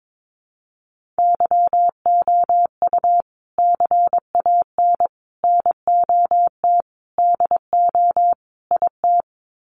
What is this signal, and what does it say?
Morse code, heard as a keyed tone, about 22 wpm, 700 Hz. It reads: YOU CAN NOT DO IT
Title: Sets of 5 Words Encore2 22wpm